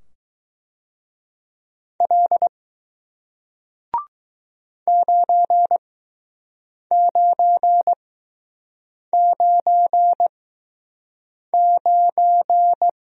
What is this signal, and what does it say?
Morse code, heard as a keyed tone, about 20 wpm, spaced as for 6 wpm, 700 Hz. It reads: L 9999